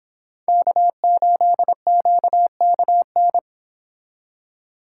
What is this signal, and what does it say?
Morse code, heard as a keyed tone, about 28 wpm, 700 Hz. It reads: K8QKN